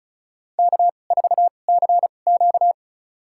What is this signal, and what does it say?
Morse code, heard as a keyed tone, about 36 wpm, spaced as for 25 wpm, 700 Hz. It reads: K4CQ
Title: USA Call Signs - 4 Characters - 2x Character Spacing - 35wpm